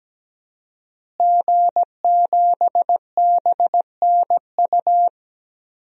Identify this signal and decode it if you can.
Morse code, heard as a keyed tone, about 17 wpm, 700 Hz. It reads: G7BNU
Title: DXCC-100 Call Signs - 5 Characters - Repeated 3x - 17wpm